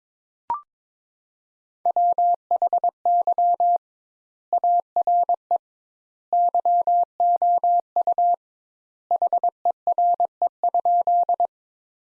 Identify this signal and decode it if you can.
Morse code, heard as a keyed tone, about 22 wpm, 700 Hz. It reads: WHY ARE YOU HERE?
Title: Sentences Top 200 Words - 22wpm